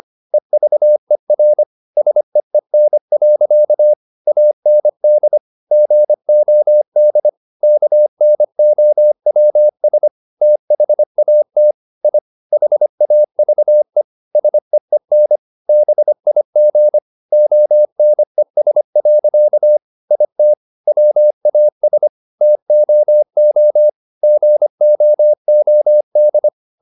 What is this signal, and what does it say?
Morse code, heard as a keyed tone, about 25 wpm, 600 Hz. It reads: EVER SEEN. AND GOD KNOWS THAT I HAVE SEEN BIG ONES. IT WAS TOO GOOD